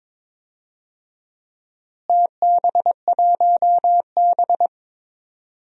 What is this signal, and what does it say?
Morse code, heard as a keyed tone, about 22 wpm, 700 Hz. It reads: TB1B
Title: DXCC-100 Call Signs - 4 Characters - Repeated 2x - 22wpm